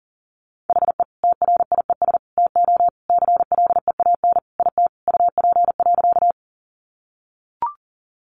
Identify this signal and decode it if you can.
Morse code, heard as a keyed tone, about 40 wpm, 700 Hz. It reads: HE TRIES TO CLEAN IT UP.
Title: Additional Easy Phrases 40wpm